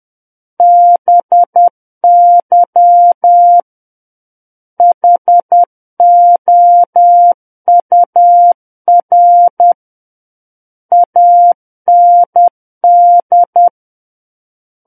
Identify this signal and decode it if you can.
Morse code, heard as a keyed tone, about 10 wpm, 700 Hz. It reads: BY HOUR AND
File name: Book One / 01-Chapter 1 - 10 WPM